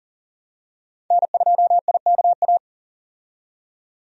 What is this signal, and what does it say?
Morse code, heard as a keyed tone, about 40 wpm, 700 Hz. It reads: N2IKA
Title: USA Call Signs - 5 Characters - 40wpm